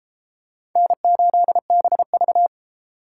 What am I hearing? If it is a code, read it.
Morse code, heard as a keyed tone, about 32 wpm, 700 Hz. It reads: N8BV